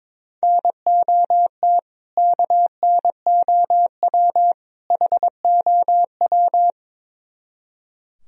Morse code, heard as a keyed tone, about 22 wpm, 700 Hz. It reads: NOT KNOW HOW
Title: Speed-Racing - Sets of 3 Words 22wpm